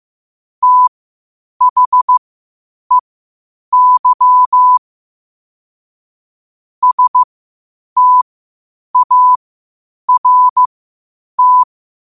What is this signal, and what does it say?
Morse code, heard as a keyed tone, about 15 wpm, spaced as for 8 wpm, 1000 Hz. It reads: THEY START
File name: qotdfiles / 8wpm